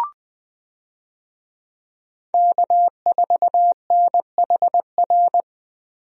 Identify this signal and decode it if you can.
Morse code, heard as a keyed tone, about 20 wpm, 700 Hz. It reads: K4NHR
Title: USA Call Signs - 5 Characters - Repeated 3x - 20wpm